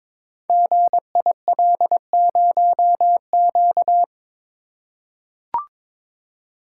Morse code, heard as a keyed tone, about 22 wpm, 700 Hz. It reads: GIL0Q
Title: Code Groups - 5 Letter:Numbers - 22wpm